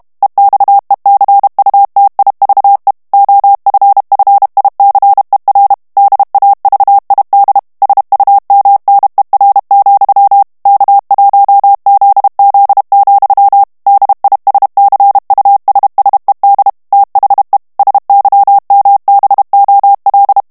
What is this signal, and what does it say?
Morse code, heard as a keyed tone, about 32 wpm, 800 Hz. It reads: EXECUTIVE OFFICER DAVID SUMNER, K1ZZ, DISCUSSED THE SYMBOL
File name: ANGLAIS texte clair 3 vitesse 30 mots minute